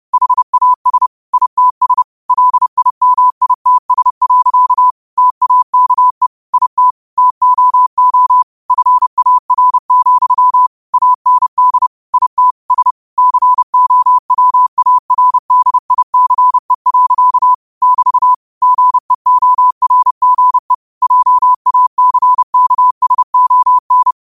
Morse code, heard as a keyed tone, about 30 wpm, 1000 Hz. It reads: HAS ITS LIMITS. TAKE IT TOO FAR, AND IT S COWARDICE. <BT> GEORGE JACKSON
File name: qotdfiles / 30wpm